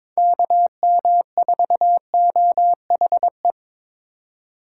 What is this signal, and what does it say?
Morse code, heard as a keyed tone, about 25 wpm, 700 Hz. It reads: KM4OHE
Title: Speed Racing - USA 2x3 Call Signs 22wpm